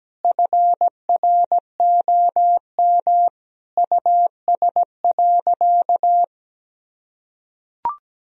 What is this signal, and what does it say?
Morse code, heard as a keyed tone, about 17 wpm, 700 Hz. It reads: FROM US.
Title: Sentences from Top 1000 Words - Review Part 1 - 17wpm